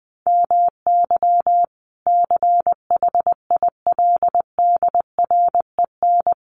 Morse code, heard as a keyed tone, about 20 wpm, 700 Hz. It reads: MY CHILDREN
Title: Sentences from Top 1500 Words 20wpm